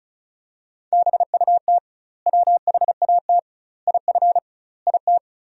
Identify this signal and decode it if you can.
Morse code, heard as a keyed tone, about 36 wpm, 700 Hz. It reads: BUT WHAT IF IT
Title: 2x Word-Spacing - Sentences from Top 100 Words - 35wpm